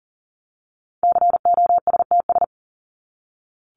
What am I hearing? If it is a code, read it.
Morse code, heard as a keyed tone, about 39 wpm, 700 Hz. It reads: COSTS
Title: ES:S Suffix - Top 100 Words with Base Words 40wpm